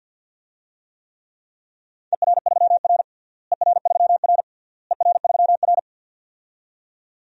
Morse code, heard as a keyed tone, about 50 wpm, 700 Hz. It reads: ER3R ER3R ER3R
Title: DXCC-100 Call Signs - 4 Characters - Repeated 3x - 50wpm